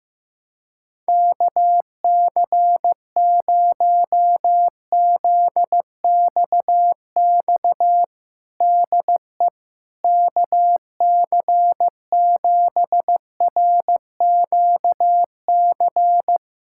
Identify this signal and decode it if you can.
Morse code, heard as a keyed tone, about 15 wpm, 700 Hz. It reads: KC0ZXX DE KC7RQC
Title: USA Callsign DE Callsign 15wpm